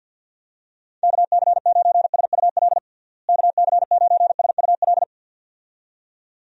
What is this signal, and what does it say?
Morse code, heard as a keyed tone, about 52 wpm, 700 Hz. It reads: KC9SUL KC9SUL
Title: USA Call Signs - 6 Characters - Repeated 2x - 50wpm